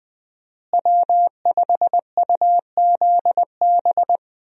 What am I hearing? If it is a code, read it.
Morse code, heard as a keyed tone, about 20 wpm, 700 Hz. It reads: W5UZB
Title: USA Call Signs - 5 Characters - Repeated 3x - 20wpm